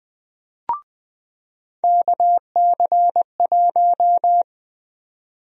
Morse code, heard as a keyed tone, about 21 wpm, 700 Hz. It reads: KC1
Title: USA Call Sign Prefixes - 20wpm